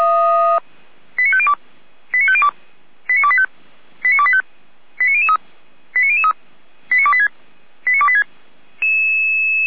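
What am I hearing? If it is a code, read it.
Five-tone selective call (ZVEI), heard as quick runs of five tones, tones of about 70 ms. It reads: 89502 89502 89285 89285 89003 89003 89286 89286
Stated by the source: ZVEI five-tone selective call received by SDR, logged by Feuerwehr Orlamünde (fire brigade).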